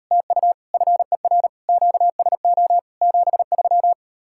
Morse code, heard as a keyed tone, about 38 wpm, 700 Hz. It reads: TU FER QSO 73